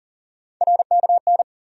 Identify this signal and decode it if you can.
Morse code, heard as a keyed tone, about 40 wpm, 700 Hz. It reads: RKN
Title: USA Call Sign Postfixes - 3 Letters 40wpm